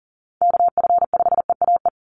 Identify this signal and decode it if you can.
Morse code, heard as a keyed tone, about 40 wpm, 700 Hz. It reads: KF5EAE
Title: USA Call Signs 40wpm